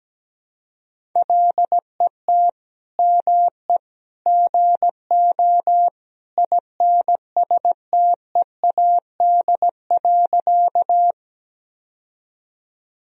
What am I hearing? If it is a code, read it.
Morse code, heard as a keyed tone, about 17 wpm, 700 Hz. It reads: LET ME GO INSTEAD.
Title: Sentences from Top 1000 Words - Review Part 4 - 17wpm